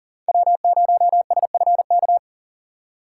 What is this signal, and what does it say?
Morse code, heard as a keyed tone, about 40 wpm, 700 Hz. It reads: W0SFK
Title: USA Call Signs - 5 Characters - 3x Character Spacing - 40wpm